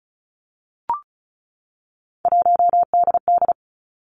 Morse code, heard as a keyed tone, about 36 wpm, 700 Hz. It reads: 1DD
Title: Three Challenging Letter-Number Combinations 35wpm